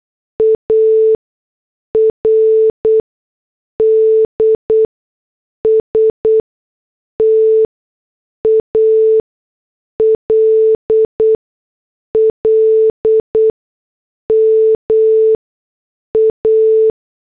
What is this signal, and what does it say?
Morse code, heard as a keyed tone, about 8 wpm, spaced as for 6 wpm, 440 Hz. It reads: ARDS TALL MA